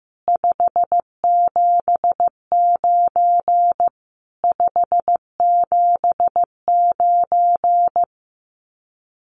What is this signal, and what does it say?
Morse code, heard as a keyed tone, about 15 wpm, 700 Hz. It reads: 579 579
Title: Common QSO Phrases Encore 15wpm